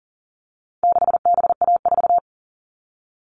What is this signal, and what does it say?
Morse code, heard as a keyed tone, about 40 wpm, 700 Hz. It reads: 6BA4
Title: All Cut Numbers - 4-Digit 40wpm